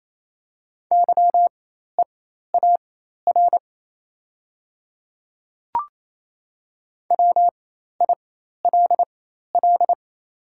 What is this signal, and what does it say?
Morse code, heard as a keyed tone, about 27 wpm, spaced as for 13 wpm, 690 Hz. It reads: YEAR WILL
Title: 4x Character Spacing - Top 200 Words - 28wpm